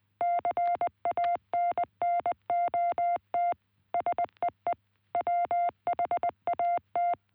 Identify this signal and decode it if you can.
Morse code, heard as a keyed tone, about 18 wpm, 660 Hz. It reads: CANNOT SEE WHAT